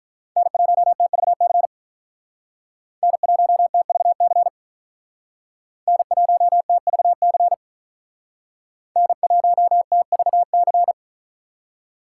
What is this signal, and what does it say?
Morse code, heard as a keyed tone, about 46 wpm, 700 Hz. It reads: N1TVC N1TVC N1TVC N1TVC